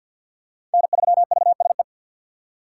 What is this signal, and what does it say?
Morse code, heard as a keyed tone, about 50 wpm, 700 Hz. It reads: N3VSE
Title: USA Call Signs - 5 Characters - 3x Character Spacing - 50wpm